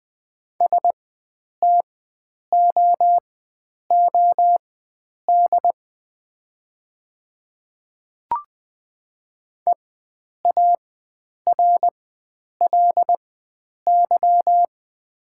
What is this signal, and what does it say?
Morse code, heard as a keyed tone, about 20 wpm, spaced as for 9 wpm, 700 Hz. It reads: STOOD EARLY
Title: 4x Character Spacing - Top 500 Words - 20wpm